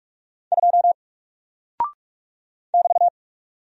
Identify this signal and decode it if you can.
Morse code, heard as a keyed tone, about 44 wpm, 700 Hz. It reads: 2 <BT>
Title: Single Character (Letter:Number:Punctuation:Prosign) 45wpm